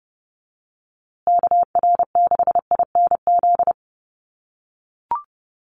Morse code, heard as a keyed tone, about 30 wpm, 700 Hz. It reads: KR6INZ
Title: USA Call Signs Repeated 2x 30wpm